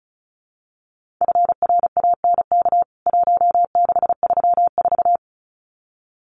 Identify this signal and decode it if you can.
Morse code, heard as a keyed tone, about 34 wpm, 700 Hz. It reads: FRANK 1634
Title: CWT Exchange 35wpm